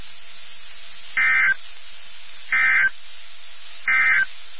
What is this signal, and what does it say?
Emergency Alert System (SAME) tones, heard as harsh buzzing bursts. It is heard from 1.0 to 4.3 s.
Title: NOAA Weather Radio WXK79